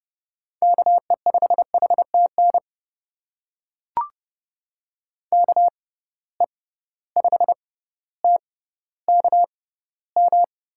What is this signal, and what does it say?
Morse code, heard as a keyed tone, about 31 wpm, spaced as for 15 wpm, 700 Hz. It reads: KE5HTN KE5TKM
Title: USA Call Signs - 6 Characters - 6x Character Spacing - 30wpm